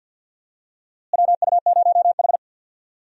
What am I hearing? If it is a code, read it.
Morse code, heard as a keyed tone, about 51 wpm, 700 Hz. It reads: WU0H
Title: USA Call Signs - 4 Characters - Repeated 2x - 50wpm